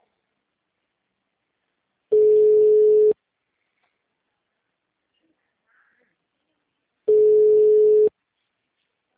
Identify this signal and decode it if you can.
Ringback tone (Europe, 425 Hz).
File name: out- -2015- - - .163